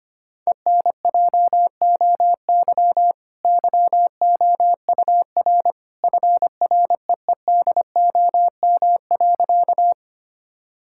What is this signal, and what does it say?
Morse code, heard as a keyed tone, about 25 wpm, 700 Hz. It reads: ENJOY YOUR FREEDOM.
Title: Sentences from Top 1000 Words - Review Part 5 - 25wpm